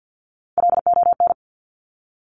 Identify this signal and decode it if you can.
Morse code, heard as a keyed tone, about 50 wpm, 700 Hz. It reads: LON
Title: Top Three Letter Combinations in Words 50wpm